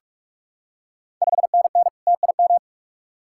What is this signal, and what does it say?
Morse code, heard as a keyed tone, about 43 wpm, 700 Hz. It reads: 5NN TIM
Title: Ontario QSO Party Exchange 45wpm